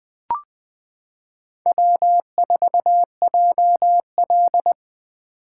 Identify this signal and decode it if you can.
Morse code, heard as a keyed tone, about 20 wpm, 700 Hz. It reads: W4JL
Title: 2023 SOTA Chaser Call Signs - 20wpm